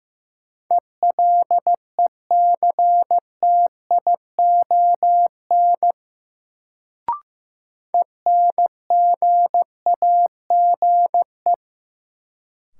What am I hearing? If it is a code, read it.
Morse code, heard as a keyed tone, about 15 wpm, 700 Hz. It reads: ELECTION ENGAGE
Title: Sentences from Top 1500 Words - New Word Drill - 15wpm